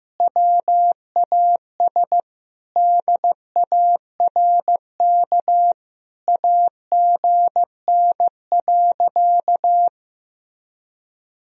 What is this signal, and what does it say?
Morse code, heard as a keyed tone, about 16 wpm, 700 Hz. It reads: WAS DARK AGN.
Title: Sentences from Top 500 Words and Abbreviations 15wpm